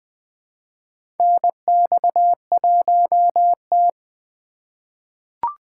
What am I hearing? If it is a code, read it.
Morse code, heard as a keyed tone, about 21 wpm, 700 Hz. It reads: NX1T
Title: USA Call Signs - 2x Character Spacing - 20wpm